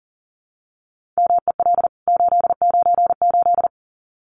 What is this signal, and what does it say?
Morse code, heard as a keyed tone, about 40 wpm, 700 Hz. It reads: MEL 898
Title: CWT Exchange 40wpm